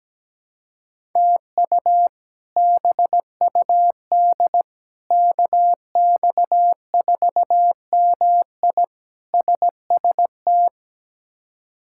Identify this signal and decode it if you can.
Morse code, heard as a keyed tone, about 17 wpm, 700 Hz. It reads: TU BUD KX4MI SST
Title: K1USN SST QSO Practice - 17wpm